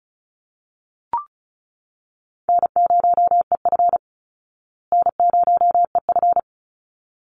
Morse code, heard as a keyed tone, about 34 wpm, 700 Hz. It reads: N0EF N0EF
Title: Easy Call Signs Repeated 2x - 35wpm